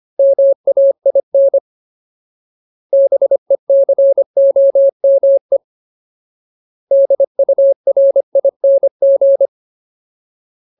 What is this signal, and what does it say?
Morse code, heard as a keyed tone, about 25 wpm, 550 Hz. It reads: MAIN BECOME DURING
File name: Ultimate Word List / 25WPM Ultimate